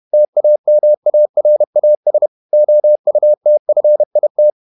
Morse code, heard as a keyed tone, about 31 wpm, 600 Hz. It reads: TAMARAS OUTFIT
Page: Never 31wpm | CW med Gnister